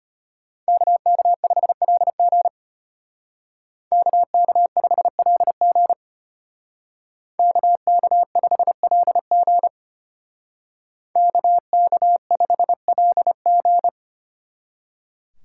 Morse code, heard as a keyed tone, about 32 wpm, 700 Hz. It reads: KK5LG KK5LG KK5LG KK5LG